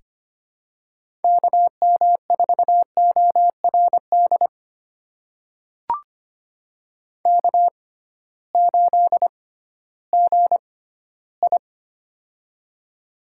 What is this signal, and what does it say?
Morse code, heard as a keyed tone, about 25 wpm, spaced as for 15 wpm, 700 Hz. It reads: KM4ORD K8GI
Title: USA Call Signs - 6x Character Spacing - 25wpm